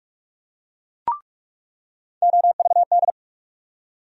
Morse code, heard as a keyed tone, about 47 wpm, 700 Hz. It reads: OVD
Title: USA Call Sign Postfixes - 3 Letters 45wpm